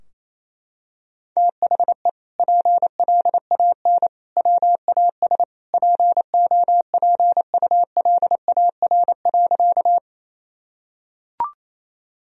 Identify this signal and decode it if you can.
Morse code, heard as a keyed tone, about 28 wpm, 700 Hz. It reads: THE PLAN WAS POPULAR.
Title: Sentences from Top 1000 Words - Review Part 3 - 28wpm